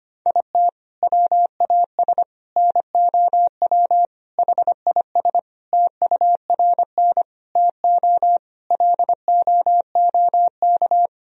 Morse code, heard as a keyed tone, about 25 wpm, 700 Hz. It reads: IT WAS NOW HIS TURN TO LOOK
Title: Sentences from Top 200 Words and Abbreviations 25wpm